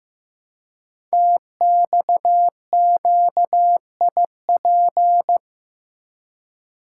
Morse code, heard as a keyed tone, about 15 wpm, 700 Hz. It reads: TXQIP
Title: Code Groups - 5 Letters - 15wpm